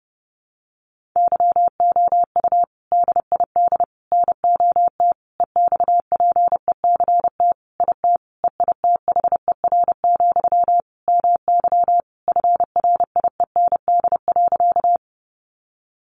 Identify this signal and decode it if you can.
Morse code, heard as a keyed tone, about 30 wpm, 700 Hz. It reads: YOU DID NOT EXPECT IT EITHER, MY FRIEND.
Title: Sentences from Top 1100 Words 30wpm